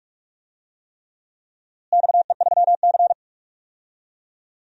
Morse code, heard as a keyed tone, about 41 wpm, 700 Hz. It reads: XE3C
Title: DXCC-100 Call Signs - 4 Characters - Repeated 3x - 45wpm